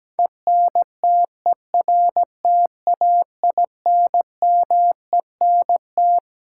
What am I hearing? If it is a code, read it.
Morse code, heard as a keyed tone, about 17 wpm, 700 Hz. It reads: ENTERTAINMENT
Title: Sentences from Top 1000 Words - Review Part 6 - New Word Drill - 17wpm